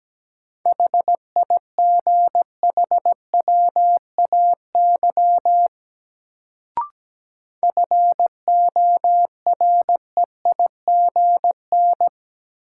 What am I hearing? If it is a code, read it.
Morse code, heard as a keyed tone, about 17 wpm, 700 Hz. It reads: HIGHWAY FOREIGN
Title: Sentences from Top 1400 Words - New Word Drill Alt - 17wpm